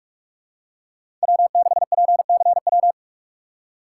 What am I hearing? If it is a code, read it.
Morse code, heard as a keyed tone, about 44 wpm, 700 Hz. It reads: W6PCW
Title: USA Call Signs - 5 Characters - 4x Character Spacing - 45wpm